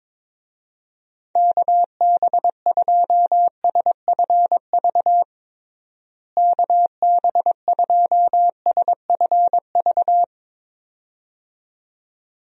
Morse code, heard as a keyed tone, about 21 wpm, 700 Hz. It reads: KB2SFV KB2SFV
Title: USA Call Signs - 6 Characters - Repeated 2x - 22wpm